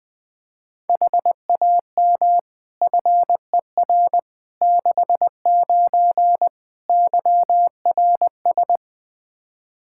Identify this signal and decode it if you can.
Morse code, heard as a keyed tone, about 20 wpm, 700 Hz. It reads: HAM FER 69 YRS
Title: Practice QSO - Exchange 3 - More Info - 20wpm